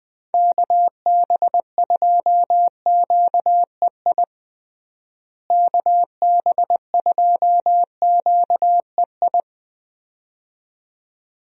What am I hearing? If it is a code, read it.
Morse code, heard as a keyed tone, about 21 wpm, 700 Hz. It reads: KB2QEI KB2QEI
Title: USA Call Signs - 6 Characters - Repeated 2x - 20wpm